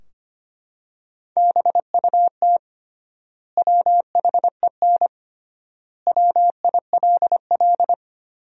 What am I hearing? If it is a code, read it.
Morse code, heard as a keyed tone, about 25 wpm, 700 Hz. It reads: BUT WHEN WILL
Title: 3x Word-Spacing - Sentences from Top 100 Words - 25wpm